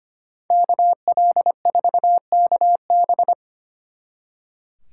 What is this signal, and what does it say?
Morse code, heard as a keyed tone, about 30 wpm, 700 Hz. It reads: KL4KB
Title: Speed Racing - USA 2x2 Call Signs 25wpm